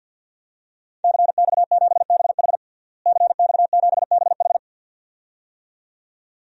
Morse code, heard as a keyed tone, about 50 wpm, 700 Hz. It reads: CX7BH CX7BH
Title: DXCC-100 Call Signs - 5 Characters - Repeated 2x - 50wpm